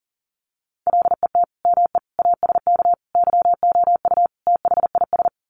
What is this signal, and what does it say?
Morse code, heard as a keyed tone, about 40 wpm, 700 Hz. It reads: LET ME ASK YOU THIS
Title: Sets of 5 Words 40wpm